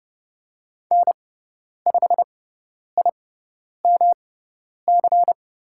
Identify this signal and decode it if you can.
Morse code, heard as a keyed tone, about 30 wpm, spaced as for 10 wpm, 700 Hz. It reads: N5IMC
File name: Lesson 18g - Letter m - Call Signs 30wpm